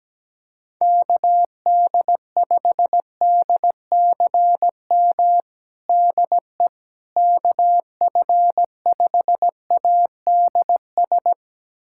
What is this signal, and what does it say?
Morse code, heard as a keyed tone, about 17 wpm, 700 Hz. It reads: KD5DCM DE KF5ADS
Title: USA Callsign DE Callsign 17wpm